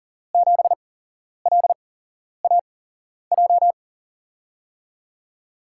Morse code, heard as a keyed tone, about 39 wpm, spaced as for 11 wpm, 700 Hz. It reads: 7LAJ
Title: USA Call Signs - 5 Characters - 8x Character Spacing - 40wpm